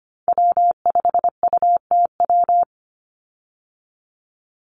Morse code, heard as a keyed tone, about 25 wpm, 700 Hz. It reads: W5UTW
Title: Call Signs Repeated 3x - 25wpm